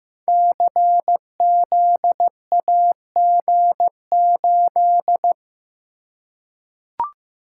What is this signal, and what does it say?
Morse code, heard as a keyed tone, about 15 wpm, 700 Hz. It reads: CZAG8
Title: Code Groups - 5 Letter:Numbers - 15wpm